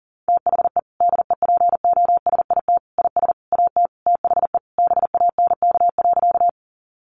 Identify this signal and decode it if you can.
Morse code, heard as a keyed tone, about 37 wpm, 700 Hz. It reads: THE DEPOSIT IS AT THE BANK.
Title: Sentences from Top 1300 Words 40wpm